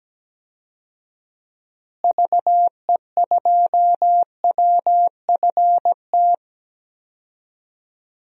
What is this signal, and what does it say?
Morse code, heard as a keyed tone, about 17 wpm, 700 Hz. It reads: VE2WFT
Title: DXCC-100 Call Signs - 6 Characters - Repeated 3x - 17wpm